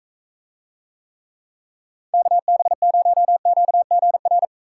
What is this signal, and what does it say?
Morse code, heard as a keyed tone, about 44 wpm, 700 Hz. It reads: KB0QGR